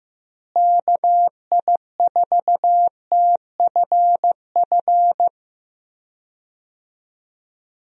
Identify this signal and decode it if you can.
Morse code, heard as a keyed tone, about 15 wpm, 700 Hz. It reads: KI4TFF
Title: USA Call Signs - 6 Characters - Repeated 3x - 15wpm